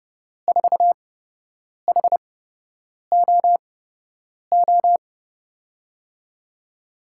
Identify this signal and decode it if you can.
Morse code, heard as a keyed tone, about 30 wpm, spaced as for 8 wpm, 700 Hz. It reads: 4HOO
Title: USA Call Signs - 6 Characters - 8x Character Spacing - 30wpm